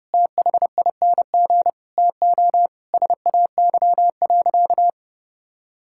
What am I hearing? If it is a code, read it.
Morse code, heard as a keyed tone, about 30 wpm, 700 Hz. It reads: THING TO SAY.
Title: Sentences from Top 1000 Words - Review Part 5 - 30wpm